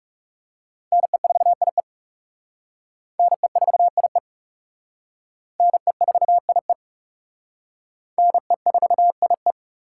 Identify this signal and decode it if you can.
Morse code, heard as a keyed tone, about 36 wpm, 700 Hz. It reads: NE4IE NE4IE NE4IE NE4IE
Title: Speed Racing - USA Call Signs 30wpm